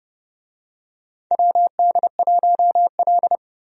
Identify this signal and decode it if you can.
Morse code, heard as a keyed tone, about 30 wpm, 700 Hz. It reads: WD1L
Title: USA Call Signs - 4 Characters - 30wpm